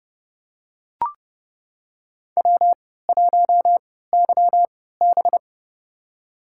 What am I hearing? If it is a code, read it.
Morse code, heard as a keyed tone, about 30 wpm, spaced as for 17 wpm, 700 Hz. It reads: W1YB
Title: USA Call Signs - 4 Characters - 3x Character Spacing - 30wpm